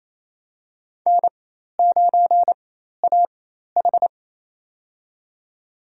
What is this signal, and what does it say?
Morse code, heard as a keyed tone, about 29 wpm, spaced as for 13 wpm, 700 Hz. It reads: N9AH
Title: USA Call Signs - 4 Characters - 4x Character Spacing - 28wpm